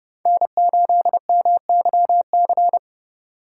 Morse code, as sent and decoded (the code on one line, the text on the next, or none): -. ---.. -- -.-- -.-.
N8MYC